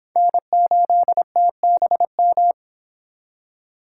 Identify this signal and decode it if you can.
Morse code, heard as a keyed tone, about 25 wpm, 700 Hz. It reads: N8TBM